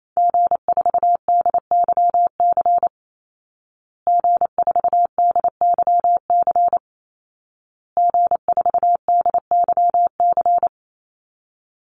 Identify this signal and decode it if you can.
Morse code, heard as a keyed tone, about 28 wpm, 700 Hz. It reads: G4DYC G4DYC G4DYC
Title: Call Signs Repeated 2x - 28wpm